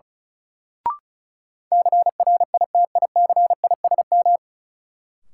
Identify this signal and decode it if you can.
Morse code, heard as a keyed tone, about 34 wpm, 700 Hz. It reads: CRITICISM
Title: Sentences from Top 1300 Words - New Word Drill - 35wpm